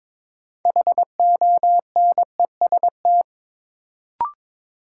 Morse code, heard as a keyed tone, about 22 wpm, 700 Hz. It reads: HONEST
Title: Sets of 5 Words Encore - New Word Drill - 22wpm